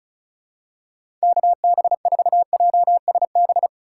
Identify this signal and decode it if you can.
Morse code, heard as a keyed tone, about 34 wpm, 700 Hz. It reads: KB4JSB
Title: USA Call Signs - 4x Character Spacing - 35wpm